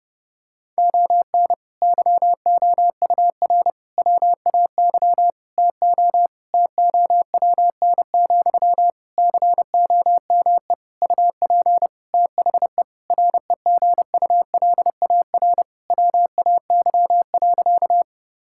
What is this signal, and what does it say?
Morse code, heard as a keyed tone, about 30 wpm, 700 Hz. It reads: ON YOUR WAY TO TOWN, COME UP THE REGULAR WAY.
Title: Sentences from Top 1000 Words - Review Part 6 - 30wpm